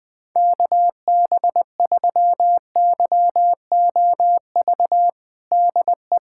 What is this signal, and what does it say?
Morse code, heard as a keyed tone, about 20 wpm, 700 Hz. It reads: KB3YOV DE
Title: USA Callsign DE Callsign - 6 Characters - 20wpm